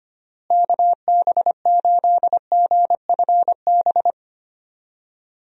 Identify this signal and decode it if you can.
Morse code, heard as a keyed tone, about 25 wpm, 700 Hz. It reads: KB8GFB
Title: USA Call Signs - 6 Characters - Repeated 2x - 25wpm